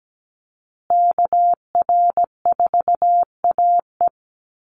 Morse code, heard as a keyed tone, about 17 wpm, 700 Hz. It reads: KR4AE
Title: Call Signs Repeated 2x - 17wpm